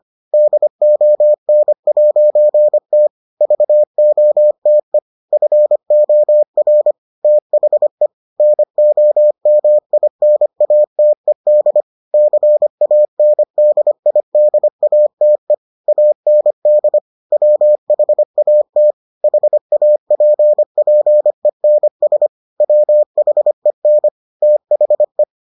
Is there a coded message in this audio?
Yes: Morse code from 0.3 to 25.2 s, heard as a keyed tone, about 25 wpm, 600 Hz. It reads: DON'T VOTE FOR THE NOMINATED CANDIDATE AND WHAT HAPPENS WHEN THE